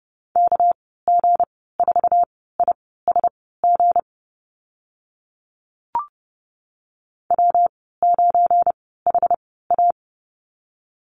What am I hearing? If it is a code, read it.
Morse code, heard as a keyed tone, about 30 wpm, spaced as for 17 wpm, 700 Hz. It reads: KG4ISG W9HA
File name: Lesson 37g - Letter x - Call Signs 30wpm